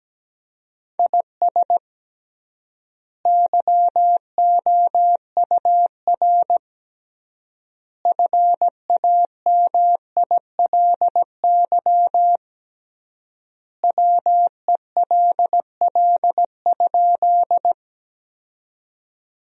Morse code, heard as a keyed tone, about 17 wpm, 700 Hz. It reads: IS YOUR FAMILY WELL?
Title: 3x Word-Spacing - Sentences from Top 400 Words - 17wpm